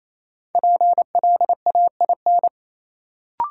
Morse code, heard as a keyed tone, about 28 wpm, 700 Hz. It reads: PLAIN
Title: Sentences from Top 400 Words - New Word Drill - 28wpm